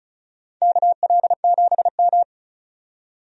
Morse code, heard as a keyed tone, about 36 wpm, 700 Hz. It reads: KL7M
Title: DXCC-100 Call Signs - 4 Characters - 35wpm